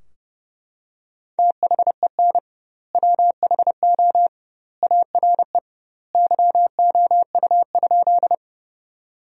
Morse code, heard as a keyed tone, about 32 wpm, 700 Hz. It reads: THEN WHO ARE YOU?
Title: 2x Word-Spacing - Sentences from Top 100 Words - 30wpm